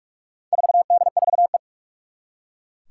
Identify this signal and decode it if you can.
Morse code, heard as a keyed tone, about 44 wpm, 700 Hz. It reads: 4D4E
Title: Four Challenging Letter-Number Combinations 45wpm